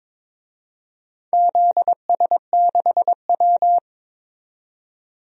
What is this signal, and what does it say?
Morse code, heard as a keyed tone, about 21 wpm, 700 Hz. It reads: ZS6W
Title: DXCC-100 Call Signs - 4 Characters - 22wpm